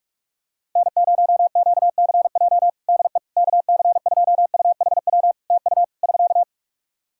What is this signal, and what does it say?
Morse code, heard as a keyed tone, about 46 wpm, 700 Hz. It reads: N0QCJ DE KC2UHW TU <SK>
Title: K1USN SST QSO Practice 45wpm